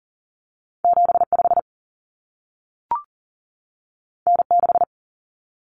Morse code, heard as a keyed tone, about 40 wpm, 700 Hz. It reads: 7 5 N 6
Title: Common Cut Numbers - 2-Digit 40wpm